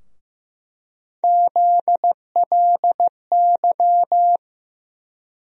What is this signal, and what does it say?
Morse code, heard as a keyed tone, about 15 wpm, 700 Hz. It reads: ZLY
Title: USA Call Sign Postfixes - 3 Letters 15wpm